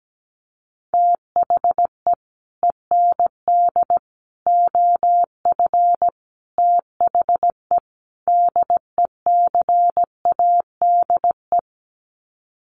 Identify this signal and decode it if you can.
Morse code, heard as a keyed tone, about 17 wpm, 700 Hz. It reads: THE END OF THE DECADE
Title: Sets of 5 Words Encore 17wpm